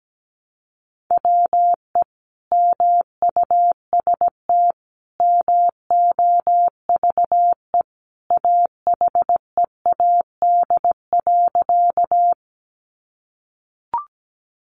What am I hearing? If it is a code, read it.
Morse code, heard as a keyed tone, about 17 wpm, 700 Hz. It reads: WE MUST MOVE AHEAD.
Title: Sentences from Top 1100 Words 17wpm